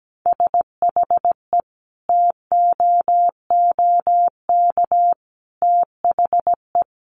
Morse code, heard as a keyed tone, about 17 wpm, 700 Hz. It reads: SHE TOOK THE
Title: Sentences from Top 1400 Words 17wpm